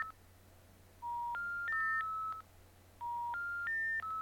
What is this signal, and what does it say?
Special information tone, with noise.
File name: pdsounds / dial-up connection failed